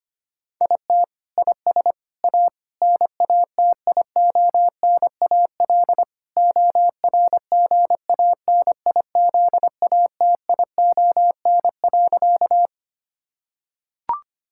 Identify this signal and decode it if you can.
Morse code, heard as a keyed tone, about 25 wpm, 700 Hz. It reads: IT IS A NATIONAL ORGANIZATION.
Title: Sentences from Top 1000 Words - Review Part 5 - 25wpm